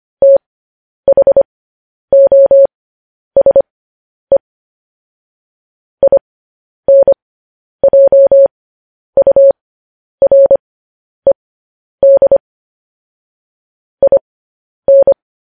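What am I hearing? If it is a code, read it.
Morse code, heard as a keyed tone, about 25 wpm, spaced as for 10 wpm, 550 Hz. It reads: THOSE INJURED IN
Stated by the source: Morse code news headlines broadcast at 10 wpm.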